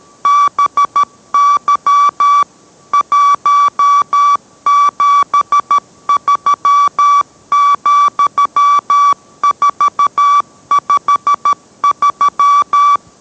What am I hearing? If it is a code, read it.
Morse code, heard as a keyed tone, about 13 wpm, 1150 Hz. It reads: BY 173,453